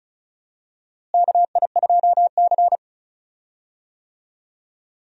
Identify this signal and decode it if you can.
Morse code, heard as a keyed tone, about 36 wpm, 700 Hz. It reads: KI2C
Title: USA Call Signs - 4 Characters - Repeated 2x - 35wpm